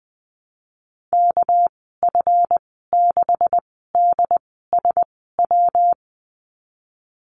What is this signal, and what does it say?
Morse code, heard as a keyed tone, about 20 wpm, spaced as for 14 wpm, 700 Hz. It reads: KF6DSW
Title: 245 Lesson 39g - Letter z - Call Signs 20wpm